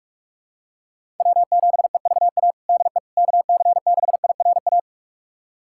Morse code, heard as a keyed tone, about 42 wpm, 700 Hz. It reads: W7EVA DE KC6IRA
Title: USA Callsign DE Callsign 45wpm